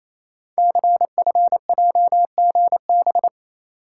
Morse code, heard as a keyed tone, about 27 wpm, 700 Hz. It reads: CFJGB
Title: Code Groups - 5 Letters - 28wpm